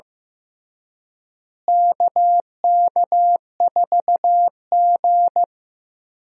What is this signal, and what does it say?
Morse code, heard as a keyed tone, about 15 wpm, 700 Hz. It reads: KK4G
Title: USA Call Signs - 4 Characters - Repeated 3x - 15wpm